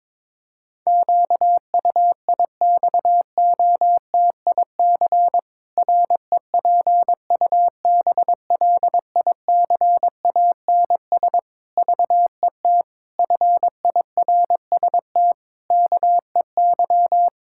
Morse code, heard as a keyed tone, about 21 wpm, 700 Hz. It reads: QUIXOTIC REPUBLICANS VET FIRST KEY
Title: Pangram Challenge v2 - 22wpm